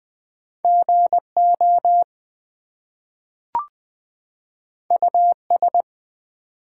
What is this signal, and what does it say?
Morse code, heard as a keyed tone, about 20 wpm, 700 Hz. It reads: GO US
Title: Two Letter Words - 20wpm